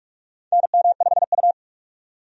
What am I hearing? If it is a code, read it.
Morse code, heard as a keyed tone, about 45 wpm, 700 Hz. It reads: NM5U
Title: USA Call Signs - 4 Characters - 45wpm